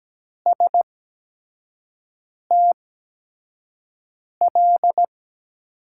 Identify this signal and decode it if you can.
Morse code, heard as a keyed tone, about 17 wpm, spaced as for 5 wpm, 700 Hz. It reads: STL
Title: USA Call Signs - 6 Characters - 8x Character Spacing - 17wpm